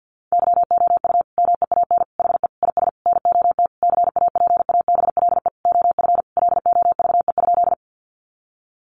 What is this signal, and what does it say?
Morse code, heard as a keyed tone, about 50 wpm, 700 Hz. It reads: YOU MEAN HE IS NOT CAPABLE OF LOVE?
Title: Sentences from Top 1200 Words 50wpm